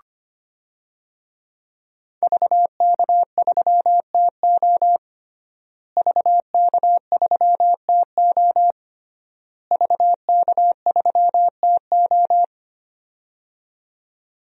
Morse code, heard as a keyed tone, about 25 wpm, 700 Hz. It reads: VK3TO VK3TO VK3TO
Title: DXCC-100 Call Signs - 5 Characters - Repeated 3x - 25wpm